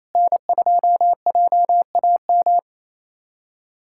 Morse code, heard as a keyed tone, about 28 wpm, 700 Hz. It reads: N2JAM
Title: USA Callsign DE Callsign - 5 Characters - 28wpm